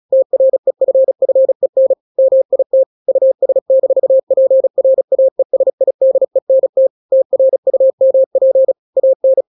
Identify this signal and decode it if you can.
Morse code, heard as a keyed tone, about 36 wpm, 530 Hz. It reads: TREFFEN MIT US-PRAESIDENT TRUMP AN